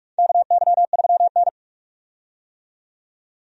Morse code, heard as a keyed tone, about 46 wpm, 700 Hz. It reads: KY3N
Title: USA Call Signs - 4 Characters - 45wpm